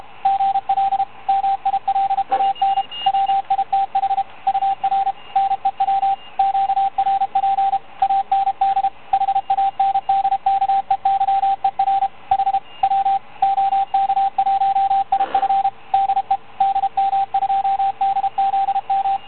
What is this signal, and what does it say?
Morse code, heard as a keyed tone, about 33 wpm, 780 Hz, with radio noise. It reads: GL MILAN WITH UR NEW QRP AND HANDKEYER H W OK1IF DE DM2DZM